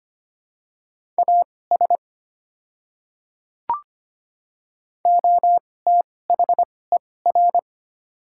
Morse code, heard as a keyed tone, about 25 wpm, spaced as for 18 wpm, 700 Hz. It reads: AS OTHER
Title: 2x Character Spacing - Top 100 Words - 25wpm